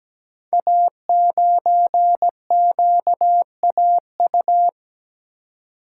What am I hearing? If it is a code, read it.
Morse code, heard as a keyed tone, about 17 wpm, 700 Hz. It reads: A9QAU
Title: Code Groups - 5 Letter:Numbers - 17wpm